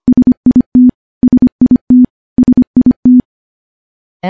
Morse code, heard as a keyed tone, about 25 wpm, 250 Hz. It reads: SIT SIT SIT
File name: morse / morse-training-25WPM